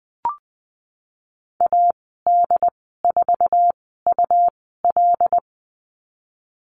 Morse code, heard as a keyed tone, about 19 wpm, spaced as for 14 wpm, 700 Hz. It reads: AD4UL
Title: 154 Lesson 25g - Letter p - Call Signs 20wpm